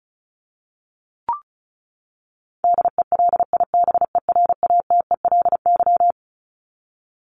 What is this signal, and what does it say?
Morse code, heard as a keyed tone, about 34 wpm, 700 Hz. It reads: DELIBERATELY
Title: Top 1751-2000 Words 35wpm